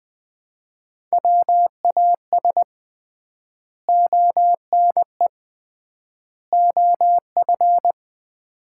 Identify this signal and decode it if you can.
Morse code, heard as a keyed tone, about 20 wpm, 700 Hz. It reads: WAS ONE OF
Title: 3x Word-Spacing - Sets of 3 Words - 20wpm